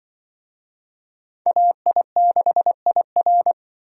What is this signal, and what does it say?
Morse code, heard as a keyed tone, about 24 wpm, 700 Hz. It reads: AI6IR
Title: Speed Racing - USA 2x2 Call Signs 20wpm